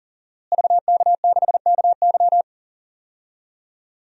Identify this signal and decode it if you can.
Morse code, heard as a keyed tone, about 37 wpm, 700 Hz. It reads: VK6KY
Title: DXCC-100 Call Signs - 5 Characters - Repeated 2x - 40wpm